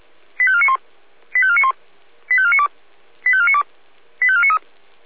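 Five-tone selective call (ZVEI), heard as quick runs of five tones, tones of about 70 ms. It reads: 85491 85491 85492 85492 85493
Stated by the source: ZVEI five-tone call received by SDR, from feuerwehr-orlamuende.org archive (fire brigade Orlamünde).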